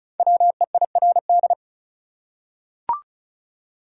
Morse code, heard as a keyed tone, about 36 wpm, 700 Hz. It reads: WEIRD
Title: Five Letter Words - 35wpm